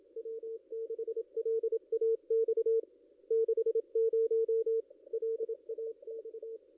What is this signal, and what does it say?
Morse code, heard as a keyed tone, about 28 wpm, 440 Hz. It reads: W6LAX 60 LAX
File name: contests / ss-cw-k5tr-op-n5rz -2200